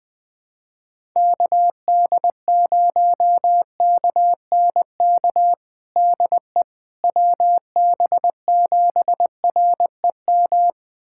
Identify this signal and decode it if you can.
Morse code, heard as a keyed tone, about 21 wpm, 700 Hz. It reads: KD0KNK DE WB7REM
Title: USA Callsign DE Callsign - 6 Characters - 20wpm